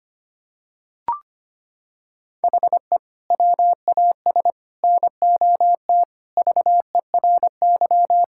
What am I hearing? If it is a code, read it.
Morse code, heard as a keyed tone, about 25 wpm, 700 Hz. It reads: HE WAS NOT VERY
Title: Sentences from Top 1000 Words - Review Part 5 - 25wpm